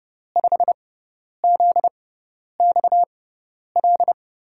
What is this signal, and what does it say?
Morse code, heard as a keyed tone, about 30 wpm, spaced as for 10 wpm, 700 Hz. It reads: 5ZXL
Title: USA Call Signs - 6 Characters - 6x Character Spacing - 30wpm